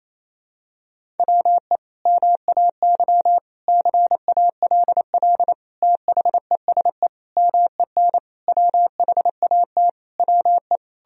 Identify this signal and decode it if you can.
Morse code, heard as a keyed tone, about 27 wpm, 700 Hz. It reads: WE MAY CALL THESE MEN WHAT WE
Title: Sentences from Top 1000 Words - Review Part 1 - 28wpm